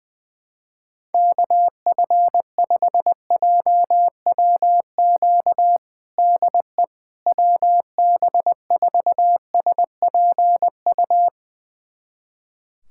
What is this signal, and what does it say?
Morse code, heard as a keyed tone, about 20 wpm, 700 Hz. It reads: KF5JWQ DE WB4SPU
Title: USA Callsign DE Callsign - 6 Characters - 20wpm